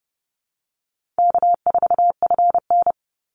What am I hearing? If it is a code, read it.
Morse code, heard as a keyed tone, about 30 wpm, 700 Hz. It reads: K4FN
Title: Easy Call Signs Repeated 3x - 30wpm